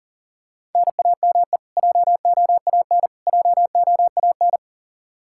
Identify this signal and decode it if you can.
Morse code, heard as a keyed tone, about 38 wpm, 700 Hz. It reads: NAME JOAN JOAN
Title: Practice QSO - Exchange 1 - Beginning - 40wpm